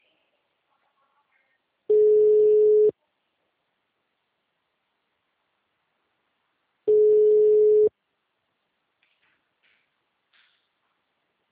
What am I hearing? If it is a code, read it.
Ringback tone (Europe, 425 Hz).
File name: out- -2006- - - .63874